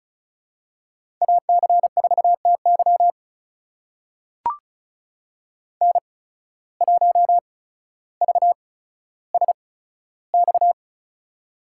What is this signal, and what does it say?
Morse code, heard as a keyed tone, about 36 wpm, spaced as for 15 wpm, 700 Hz. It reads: AC4TY N1VSX
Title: USA Call Signs - 5 Characters - 8x Character Spacing - 35wpm